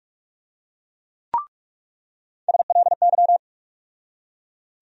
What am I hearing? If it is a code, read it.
Morse code, heard as a keyed tone, about 45 wpm, 700 Hz. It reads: SLY
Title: USA Call Sign Postfixes - 3 Letters 45wpm